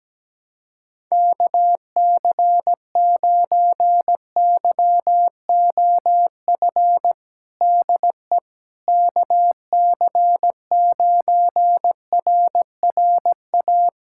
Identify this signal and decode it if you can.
Morse code, heard as a keyed tone, about 17 wpm, 700 Hz. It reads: KC9YOF DE KC9RRA
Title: USA Callsign DE Callsign 17wpm